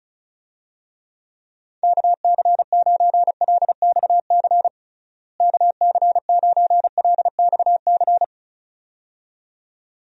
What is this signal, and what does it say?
Morse code, heard as a keyed tone, about 36 wpm, 700 Hz. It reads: KC9LXC KC9LXC
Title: USA Call Signs - 6 Characters - Repeated 2x - 35wpm